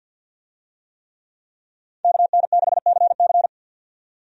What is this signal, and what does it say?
Morse code, heard as a keyed tone, about 47 wpm, 700 Hz. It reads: KN6CC
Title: USA Call Signs - 5 Characters - 50wpm